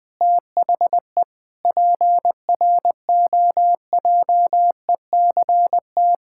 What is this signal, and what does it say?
Morse code, heard as a keyed tone, about 19 wpm, 700 Hz. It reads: THE PROJECT
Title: Sentences with 8-Word Phrases - Spoken Beginning - 20wpm